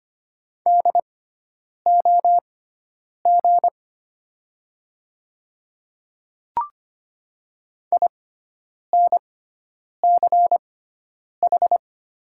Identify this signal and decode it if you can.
Morse code, heard as a keyed tone, about 25 wpm, spaced as for 9 wpm, 700 Hz. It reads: DOG INCH
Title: 6x Character Spacing - Top 500 Words - 25wpm